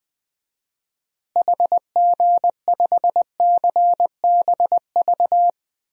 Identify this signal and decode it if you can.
Morse code, heard as a keyed tone, about 20 wpm, 700 Hz. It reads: HG5CBV
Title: DXCC-100 Call Signs - 6 Characters - Repeated 2x - 20wpm